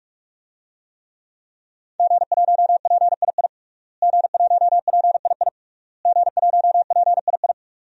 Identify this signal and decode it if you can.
Morse code, heard as a keyed tone, about 45 wpm, 700 Hz. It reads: G1PII G1PII G1PII
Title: DXCC-100 Call Signs - 5 Characters - Repeated 3x - 45wpm